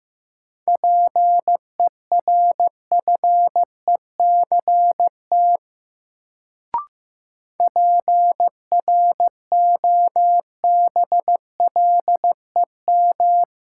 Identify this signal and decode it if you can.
Morse code, heard as a keyed tone, about 15 wpm, 700 Hz. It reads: PERFECT PROBLEM
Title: Sentences from Top 1000 Words - New Word Drill - 15wpm